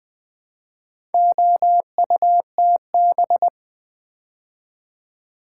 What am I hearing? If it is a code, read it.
Morse code, heard as a keyed tone, about 20 wpm, 700 Hz. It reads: OUTB
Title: Build A Word - 2 Letters At A Time - 8-Letter Words - 20wpm